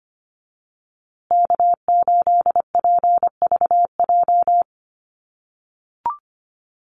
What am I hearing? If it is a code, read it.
Morse code, heard as a keyed tone, about 25 wpm, 700 Hz. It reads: K8PVJ
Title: USA Call Signs Repeated 3x 25wpm